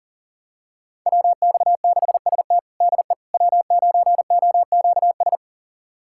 Morse code, heard as a keyed tone, about 40 wpm, 700 Hz. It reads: WX6ST DE W9OQS
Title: USA Callsign DE Callsign - 5 Characters - 40wpm